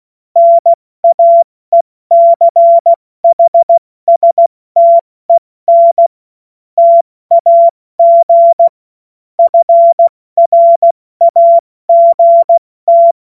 Morse code, heard as a keyed tone, about 16 wpm, 680 Hz. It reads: NAECHSTEN TAG FRAGT
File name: Morsekurs / Witze / Witz 21. September 2021, 70 F80